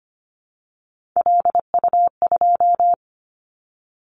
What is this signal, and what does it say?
Morse code, heard as a keyed tone, about 25 wpm, 700 Hz. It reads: LU2
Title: Callsign Prefixes for Top 100 DXCC - 25wpm